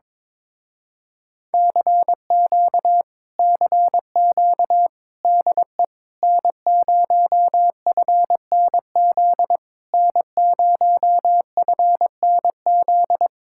Morse code, heard as a keyed tone, about 22 wpm, 700 Hz. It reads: CQ CQ DE N0FNZ N0FNZ
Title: Speed-Racing Common QSO Phrases - 22wpm